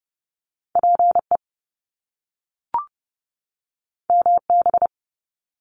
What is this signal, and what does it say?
Morse code, heard as a keyed tone, about 30 wpm, 700 Hz. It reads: PE MB
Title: Canadian Provinces and Territories 30wpm